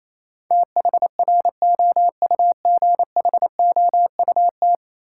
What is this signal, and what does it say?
Morse code, heard as a keyed tone, about 28 wpm, 700 Hz. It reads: THROUGHOUT
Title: Sentences from Top 800 Words - New Word Drill - 28wpm